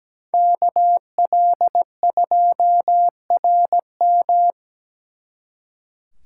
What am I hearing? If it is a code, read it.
Morse code, heard as a keyed tone, about 20 wpm, 700 Hz. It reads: KL2RM
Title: Speed Racing - USA 2x2 Call Signs 17wpm